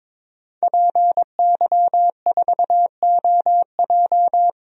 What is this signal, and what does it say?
Morse code, heard as a keyed tone, about 22 wpm, 700 Hz. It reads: PY4OJ
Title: DXCC-100 Call Signs - 5 Characters - Repeated 3x - 22wpm